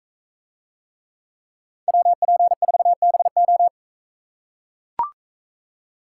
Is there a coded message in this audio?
Yes: Morse code from 1.9 to 3.7 s, heard as a keyed tone, about 42 wpm, 700 Hz. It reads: WP4BO